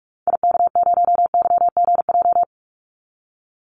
Morse code, heard as a keyed tone, about 44 wpm, 700 Hz. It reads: IK0YGJ
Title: Call Signs Repeated 2x - 45wpm